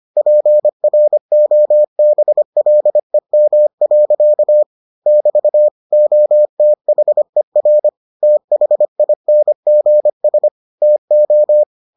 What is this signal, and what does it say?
Morse code, heard as a keyed tone, about 25 wpm, 600 Hz. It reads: PROBLEM. <BT> OTHER THINGS TO